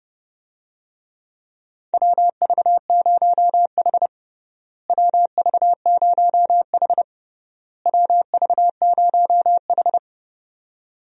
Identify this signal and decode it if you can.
Morse code, heard as a keyed tone, about 30 wpm, 700 Hz. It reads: WV0H WV0H WV0H
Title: USA Call Signs - 4 Characters - Repeated 3x - 30wpm